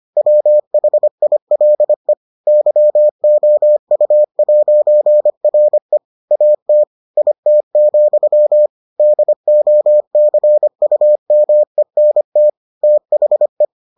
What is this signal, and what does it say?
Morse code, heard as a keyed tone, about 25 wpm, 600 Hz. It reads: WHILE YOU'RE AT IT, DOCUMENT THE